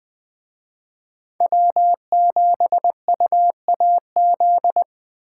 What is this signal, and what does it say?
Morse code, heard as a keyed tone, about 21 wpm, 700 Hz. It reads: W7UAZ
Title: Speed Racing - USA 1x3 Call Signs 17wpm